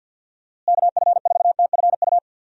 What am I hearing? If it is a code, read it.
Morse code, heard as a keyed tone, about 52 wpm, 700 Hz. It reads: KF4TFU
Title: USA Call Signs - 6 Characters - Repeated 3x - 50wpm